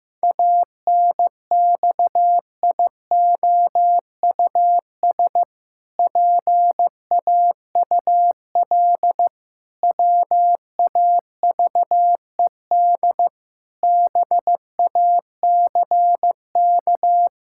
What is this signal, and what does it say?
Morse code, heard as a keyed tone, about 16 wpm, 700 Hz. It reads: ANXIOUS PAUL WAVED BACK
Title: Pangram Challenge v2 - 15wpm